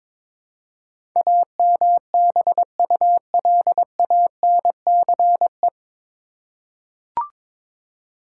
Morse code, heard as a keyed tone, about 22 wpm, 700 Hz. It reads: AMBULANCE
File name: 9 Letter Words - First 3 Letters Given - 22wpm